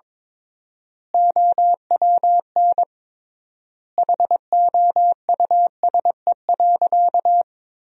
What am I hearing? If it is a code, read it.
Morse code, heard as a keyed tone, about 22 wpm, 700 Hz. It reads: OWN HOUSE.
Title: 3x Word-Spacing - Sentences from Top 300 Words - 22wpm